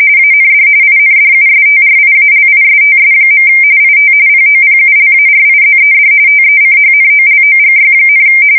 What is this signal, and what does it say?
RTTY, about 45.45 baud. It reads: AND INDUSTRIALISTS. IT POPULARIZED TECHNOLOGY,
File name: fsk170 / usnavy4